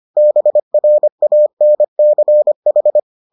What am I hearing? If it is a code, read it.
Morse code, heard as a keyed tone, about 25 wpm, 600 Hz. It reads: BRANCH